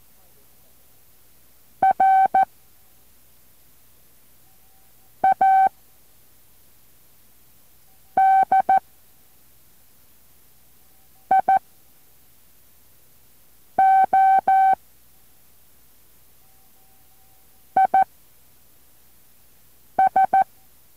Morse code, heard as a keyed tone, about 14 wpm, spaced as for 3 wpm, 770 Hz. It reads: RADIO IS